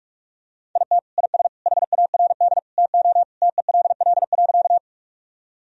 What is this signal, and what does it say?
Morse code, heard as a keyed tone, about 42 wpm, 700 Hz. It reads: IT IS HARD TO TELL.
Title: Sentences from Top 1000 Words - Review Part 1 - 45wpm